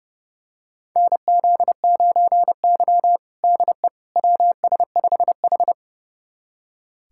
Morse code, heard as a keyed tone, about 30 wpm, 700 Hz. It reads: NZ9Y DE WS5H
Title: USA Callsign DE Callsign - 4 Characters - 30wpm